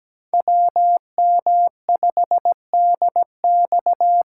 Morse code, heard as a keyed tone, about 17 wpm, 700 Hz. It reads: WM5DX
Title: K1USN SST QSO Practice - 17wpm